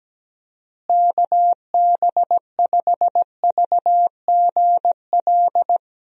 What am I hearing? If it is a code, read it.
Morse code, heard as a keyed tone, about 17 wpm, 700 Hz. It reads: KB5VGL
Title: USA Call Signs - 6 Characters - Repeated 3x - 17wpm